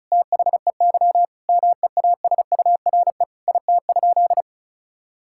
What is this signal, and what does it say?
Morse code, heard as a keyed tone, about 36 wpm, 700 Hz. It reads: THEY MEASURE IT?
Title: Sentences from Top 1000 Words - Review Part 6 - 35wpm